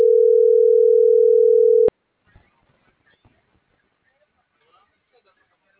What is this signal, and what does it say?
Ringback tone (North America), 440 + 480 Hz.